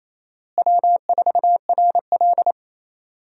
Morse code, heard as a keyed tone, about 28 wpm, 700 Hz. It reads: W4RL
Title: USA Call Signs - 4 Characters - Repeated 2x - 28wpm